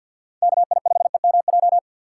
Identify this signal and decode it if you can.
Morse code, heard as a keyed tone, about 51 wpm, 700 Hz. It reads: KI5EMJ
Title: USA Call Signs - 6 Characters - 3x Character Spacing - 50wpm